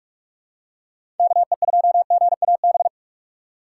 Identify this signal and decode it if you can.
Morse code, heard as a keyed tone, about 42 wpm, 700 Hz. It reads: KE2GAB
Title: USA Call Signs - 6 Characters - 8x Character Spacing - 45wpm